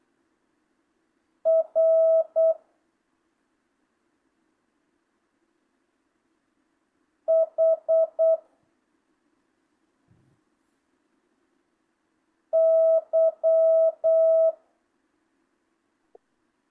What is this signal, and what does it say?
Morse code, heard as a keyed tone, about 8 wpm, 650 Hz. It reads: R H Y
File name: jh8cbh / 2025cw